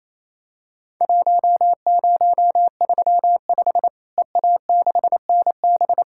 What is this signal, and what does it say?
Morse code, heard as a keyed tone, about 28 wpm, 700 Hz. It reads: 1035 EA6NB
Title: Worked All Europe DX Contest - QTC Exchange - 28wpm